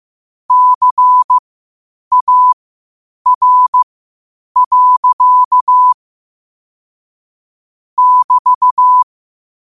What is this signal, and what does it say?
Morse code, heard as a keyed tone, about 15 wpm, spaced as for 8 wpm, 1000 Hz. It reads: CAR. <BT>
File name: qotdfiles / 8wpm